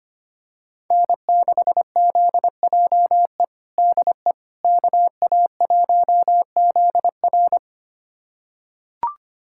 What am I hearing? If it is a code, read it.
Morse code, heard as a keyed tone, about 25 wpm, 700 Hz. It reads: N6ZJE DE KA1ZR
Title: USA Callsign DE Callsign - 5 Characters - 25wpm